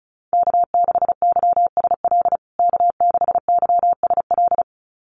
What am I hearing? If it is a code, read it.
Morse code, heard as a keyed tone, about 34 wpm, 700 Hz. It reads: K6YSL K6YSL
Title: USA Call Signs Repeated 2x 35wpm